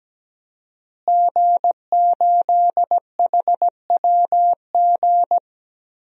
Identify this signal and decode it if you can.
Morse code, heard as a keyed tone, about 17 wpm, 700 Hz. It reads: G8HWG
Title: DXCC-100 Call Signs - 5 Characters - Repeated 3x - 17wpm